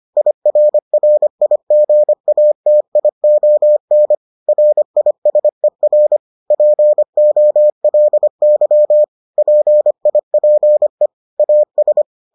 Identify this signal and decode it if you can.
Morse code, heard as a keyed tone, about 25 wpm, 600 Hz. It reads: IRRIGATION RISER POLY PIPE AS